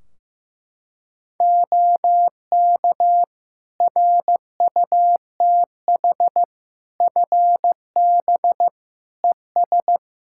Morse code, heard as a keyed tone, about 15 wpm, 700 Hz. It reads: OK RUTH FB ES
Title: Practice QSO - Exchange 2 - Info - 15wpm